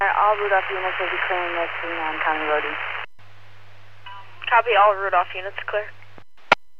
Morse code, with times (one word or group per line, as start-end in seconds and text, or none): none